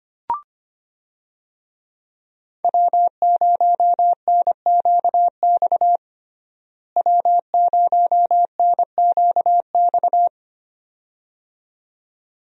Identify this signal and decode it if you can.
Morse code, heard as a keyed tone, about 25 wpm, 700 Hz. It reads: W0NQX W0NQX
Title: USA Call Signs - 5 Characters - Repeated 2x - 25wpm